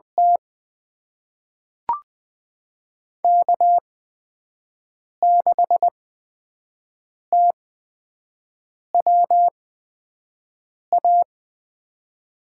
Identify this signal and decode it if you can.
Morse code, heard as a keyed tone, about 20 wpm, spaced as for 5 wpm, 700 Hz. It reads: T K6TWA
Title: USA Call Signs - 8x Character Spacing - 20wpm